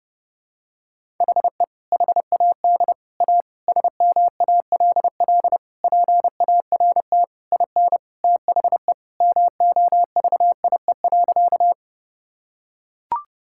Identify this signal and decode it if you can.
Morse code, heard as a keyed tone, about 30 wpm, 700 Hz. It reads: HE HAD A SMALL PART IN THE MOVIE.
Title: Sentences from Top 1000 Words - Review Part 5 - 30wpm